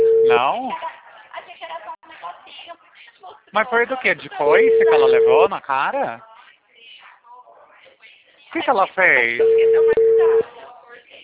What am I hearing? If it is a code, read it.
Ringback tone (China, 450 Hz), with speech.